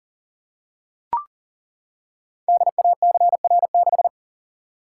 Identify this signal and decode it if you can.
Morse code, heard as a keyed tone, about 41 wpm, 700 Hz. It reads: DACR6
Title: Code Groups - 5 Letter:Numbers - 40wpm